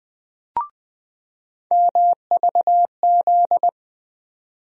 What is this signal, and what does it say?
Morse code, heard as a keyed tone, about 20 wpm, 700 Hz. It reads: MVZ
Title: USA Call Sign Postfixes - 3 Letters 20wpm